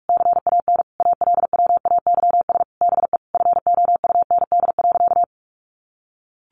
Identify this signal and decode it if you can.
Morse code, heard as a keyed tone, about 44 wpm, 700 Hz. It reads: CAN ALWAYS BE FOUND.
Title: Sentences from Top 1300 Words 45wpm